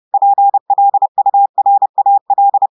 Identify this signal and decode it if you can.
Morse code, heard as a keyed tone, about 30 wpm, 800 Hz. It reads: PLURAL
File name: audios / portugais / portugais texte en clair 1 vitesse 30 mots minutes